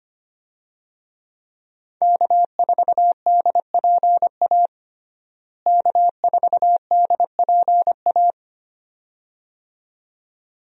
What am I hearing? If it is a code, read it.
Morse code, heard as a keyed tone, about 25 wpm, 700 Hz. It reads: K4DPA K4DPA
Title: USA Call Signs - 5 Characters - Repeated 2x - 25wpm